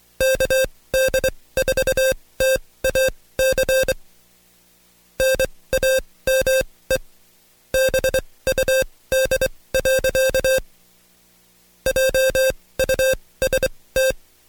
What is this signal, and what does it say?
Morse code, heard as a keyed tone, about 24 wpm, spaced as for 18 wpm, 530 Hz. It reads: KD4TAC NAME BUD. JUST
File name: Code Quick / AudioDownld2 / Seg135-Stepper8 17WPM